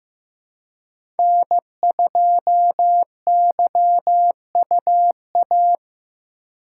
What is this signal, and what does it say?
Morse code, heard as a keyed tone, about 15 wpm, 700 Hz. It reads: N2YUA
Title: USA Call Signs - 5 Characters - 2x Character Spacing - 15wpm